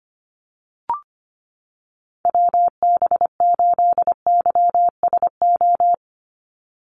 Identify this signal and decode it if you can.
Morse code, heard as a keyed tone, about 25 wpm, 700 Hz. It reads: WB8YSO
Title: USA Call Signs 25wpm